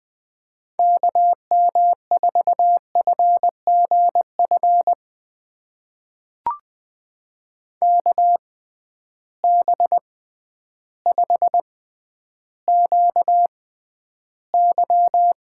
Morse code, heard as a keyed tone, about 20 wpm, spaced as for 11 wpm, 700 Hz. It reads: KM4FGF KB5QY
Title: USA Call Signs - 6 Characters - 6x Character Spacing - 20wpm